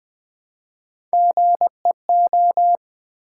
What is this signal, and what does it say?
Morse code, heard as a keyed tone, about 20 wpm, 700 Hz. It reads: GEO
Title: Build A Word - 3 Letters At A Time - 9-Letter Words - 20wpm